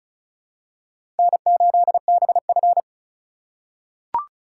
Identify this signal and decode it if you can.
Morse code, heard as a keyed tone, about 36 wpm, 700 Hz. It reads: N8BF
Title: USA Call Signs - 4x Character Spacing - 35wpm